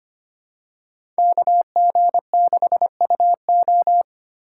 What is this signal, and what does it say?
Morse code, heard as a keyed tone, about 25 wpm, 700 Hz. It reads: KG6UO
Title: USA Call Signs - 5 Characters - 3x Character Spacing - 25wpm